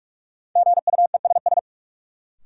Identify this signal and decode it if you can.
Morse code, heard as a keyed tone, about 46 wpm, 700 Hz. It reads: GUESS
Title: Sentences from Top 1500 Words - New Word Drill - 45wpm